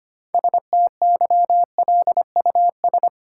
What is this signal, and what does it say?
Morse code, heard as a keyed tone, about 25 wpm, 700 Hz. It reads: STYLUS
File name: practice / 25wpm / Difficult Words - Speed-Racing with Hints - 25wpm